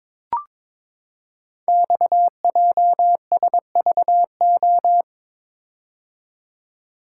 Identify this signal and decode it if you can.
Morse code, heard as a keyed tone, about 22 wpm, 700 Hz. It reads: XJSVO
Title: Code Groups - 5 Letters - 22wpm